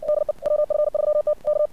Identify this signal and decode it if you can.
Morse code, heard as a keyed tone, about 59 wpm, 600 Hz. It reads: DE WF3T K
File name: archived / CRSnet